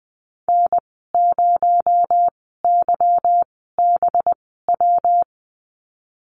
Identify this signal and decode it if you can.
Morse code, heard as a keyed tone, about 20 wpm, spaced as for 14 wpm, 700 Hz. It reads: N0YBW
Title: 219 Lesson 35g - Number 0 - Call Signs 20wpm